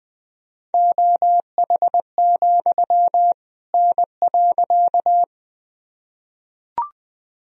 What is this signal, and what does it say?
Morse code, heard as a keyed tone, about 20 wpm, 700 Hz. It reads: OH, N.
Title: Sentences from Top 500 Words and Abbreviations 20wpm